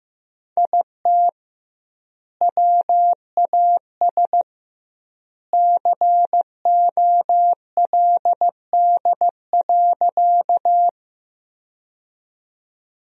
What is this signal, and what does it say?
Morse code, heard as a keyed tone, about 15 wpm, 700 Hz. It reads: IT WAS COLD.
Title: 2x Word-Spacing - Sentences from Top 400 Words - 15wpm